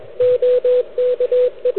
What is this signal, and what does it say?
Morse code, heard as a keyed tone, about 22 wpm, 500 Hz, with radio noise. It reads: OKE
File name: ARCHIV / QCX / NedTest